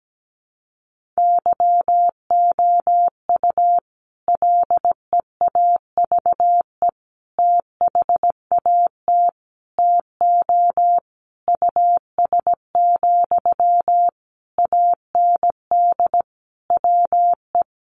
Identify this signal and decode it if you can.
Morse code, heard as a keyed tone, about 17 wpm, 700 Hz. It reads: YOU LEAVE THAT TO US, AND WE
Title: Sentences from Top 1400 Words 17wpm